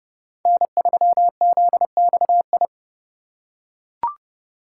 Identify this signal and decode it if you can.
Morse code, heard as a keyed tone, about 30 wpm, 700 Hz. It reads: N3ZXI
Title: USA Call Signs - 5 Characters - 2x Character Spacing - 30wpm